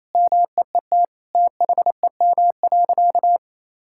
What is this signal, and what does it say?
Morse code, heard as a keyed tone, about 28 wpm, 700 Hz. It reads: MEET THEM.
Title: Sentences from Top 1000 Words - Review Part 6 - 28wpm